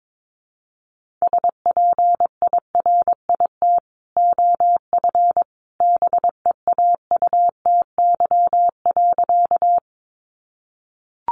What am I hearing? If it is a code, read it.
Morse code, heard as a keyed tone, about 22 wpm, 700 Hz. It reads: SPIRIT OF BEAUTY.
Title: Sentences from Top 1400 Words 22wpm